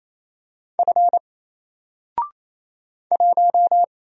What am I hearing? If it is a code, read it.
Morse code, heard as a keyed tone, about 28 wpm, 700 Hz. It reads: F 1
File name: Single-Letter-Number-28wpm